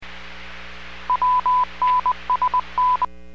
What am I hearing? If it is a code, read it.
Morse code, heard as a keyed tone, about 20 wpm, 1000 Hz. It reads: WNSN